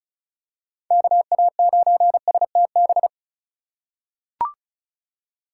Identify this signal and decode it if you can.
Morse code, heard as a keyed tone, about 34 wpm, 700 Hz. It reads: KA9STB
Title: USA Call Signs - 4x Character Spacing - 35wpm